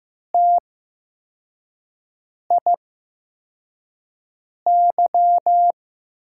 Morse code, heard as a keyed tone, about 15 wpm, spaced as for 4 wpm, 700 Hz. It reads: TIY
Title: USA Call Signs - 6 Characters - 8x Character Spacing - 15wpm